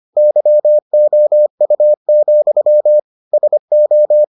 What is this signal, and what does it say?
Morse code, heard as a keyed tone, about 25 wpm, 600 Hz. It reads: YOU, SO